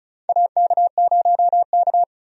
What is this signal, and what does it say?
Morse code, heard as a keyed tone, about 36 wpm, 700 Hz. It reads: AK0K
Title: USA Call Signs - 4 Characters - 35wpm